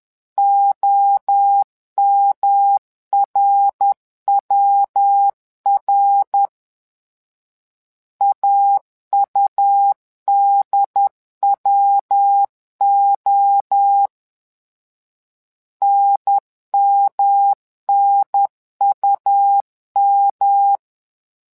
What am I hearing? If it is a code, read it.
Morse code, heard as a keyed tone, about 10 wpm, 800 Hz. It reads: OMRWR AUDWO NMNUM
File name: LEZIONI / 3 MWRO / 3 MWRO 10wpm 2148